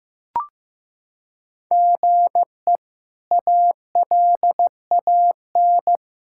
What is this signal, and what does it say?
Morse code, heard as a keyed tone, about 15 wpm, 700 Hz. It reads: GE ALAN
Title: K1USN SST QSO Practice - 15wpm